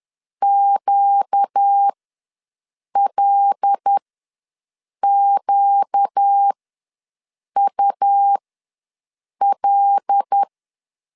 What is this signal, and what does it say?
Morse code, heard as a keyed tone, about 10 wpm, spaced as for 6 wpm, 800 Hz. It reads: QLQUL
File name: LEZIONI / 5 LBVC / 5 L ws 10wpm 3636